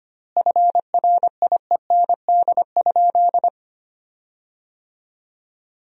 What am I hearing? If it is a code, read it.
Morse code, heard as a keyed tone, about 25 wpm, 700 Hz. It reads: FRIEND?
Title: 4x Word-Spacing - Sentences from Top 300 Words - 25wpm